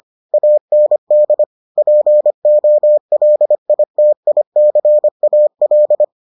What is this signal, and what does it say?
Morse code, heard as a keyed tone, about 25 wpm, 600 Hz. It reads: AND POLITICAL